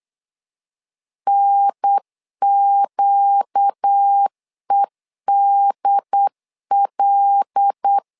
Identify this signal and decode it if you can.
Morse code, heard as a keyed tone, about 8 wpm, 800 Hz. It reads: NQEDL
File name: LEZIONI / 5 LBVC / 5 L 8wpm 2831